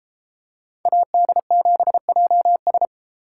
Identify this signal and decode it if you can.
Morse code, heard as a keyed tone, about 32 wpm, 700 Hz. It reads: AD7JS